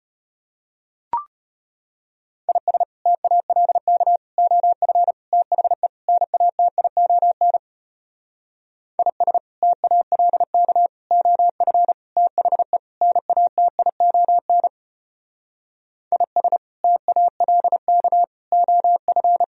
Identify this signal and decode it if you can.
Morse code, heard as a keyed tone, about 33 wpm, 700 Hz. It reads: IS TALK OF THE NATION IS TALK OF THE NATION IS TALK OF